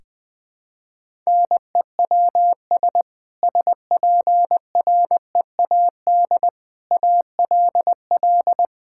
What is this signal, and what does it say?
Morse code, heard as a keyed tone, about 21 wpm, 700 Hz. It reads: NEWS SPREAD ALL
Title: Sentences with 3-Word Phrases - Spoken Beginning - 20wpm